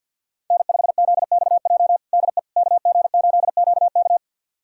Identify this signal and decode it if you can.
Morse code, heard as a keyed tone, about 53 wpm, 700 Hz. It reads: N5ZXJ DE KG8QK
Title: USA Callsign DE Callsign - 5 Characters - 50wpm